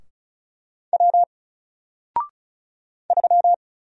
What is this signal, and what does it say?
Morse code, heard as a keyed tone, about 36 wpm, 700 Hz. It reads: W 3
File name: Single-Letter-Number-35wpm